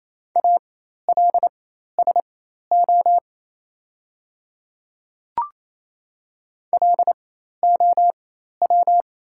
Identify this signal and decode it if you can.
Morse code, heard as a keyed tone, about 28 wpm, spaced as for 13 wpm, 690 Hz. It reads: ALSO LOW
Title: 4x Character Spacing - Top 300 Words - 28wpm